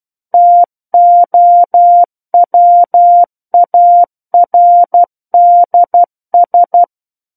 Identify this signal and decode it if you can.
Morse code, heard as a keyed tone, about 12 wpm, 700 Hz. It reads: TOWARDS
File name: Book One / 03-Chapter 3 - 12 WPM